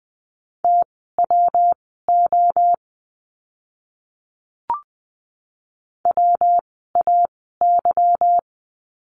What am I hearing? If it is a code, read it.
Morse code, heard as a keyed tone, about 20 wpm, spaced as for 15 wpm, 700 Hz. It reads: TWO WAY
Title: 244 Lesson 39f - Letter z - Words 20wpm